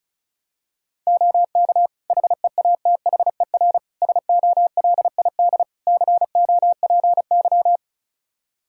Morse code, heard as a keyed tone, about 36 wpm, 700 Hz. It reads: OK HEATHER SOLID COPY
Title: K1USN SST QSO Practice 35wpm